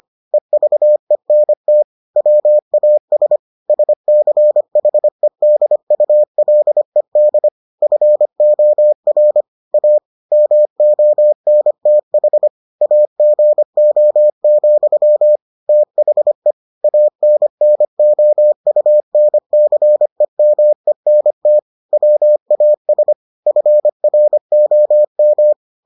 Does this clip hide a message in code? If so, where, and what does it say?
Yes: Morse code from 0.3 to 25.5 s, heard as a keyed tone, about 25 wpm, 600 Hz. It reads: EVENT WAS SCHEDULED FOR A MONTH AGO, THE ANNOUNCEMENT WAS FROM